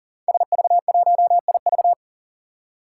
Morse code, heard as a keyed tone, about 38 wpm, 700 Hz. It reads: SV1IV
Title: Five Challenging Letter-Number Combinations 40wpm